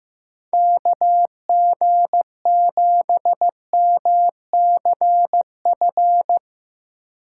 Morse code, heard as a keyed tone, about 15 wpm, 700 Hz. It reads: KG7MCF
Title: USA Call Signs - 6 Characters - 8x Character Spacing - 15wpm